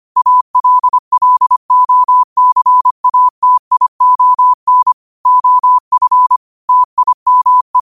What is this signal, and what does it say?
Morse code, heard as a keyed tone, about 25 wpm, 1000 Hz. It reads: ALLOCATION OF TIME
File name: qotdfiles / 25wpm